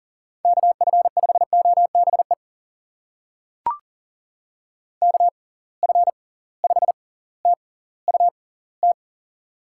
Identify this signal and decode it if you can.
Morse code, heard as a keyed tone, about 40 wpm, spaced as for 20 wpm, 700 Hz. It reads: KF5OBE KF5TUT
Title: USA Call Signs - 6 Characters - 6x Character Spacing - 40wpm